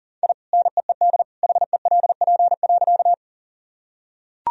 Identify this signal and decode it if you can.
Morse code, heard as a keyed tone, about 40 wpm, 700 Hz. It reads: I NEED HELP.
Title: Sentences from Top 1000 Words - Review Part 1 - 40wpm